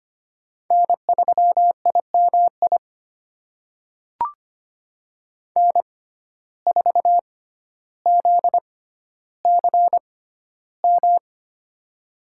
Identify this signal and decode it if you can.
Morse code, heard as a keyed tone, about 25 wpm, spaced as for 13 wpm, 700 Hz. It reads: N3IMI N4ZCM
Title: USA Call Signs - 5 Characters - 6x Character Spacing - 25wpm